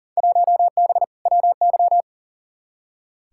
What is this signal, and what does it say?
Morse code, heard as a keyed tone, about 38 wpm, 700 Hz. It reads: 1B WY
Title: Field Day Exchanges Encore - 40wpm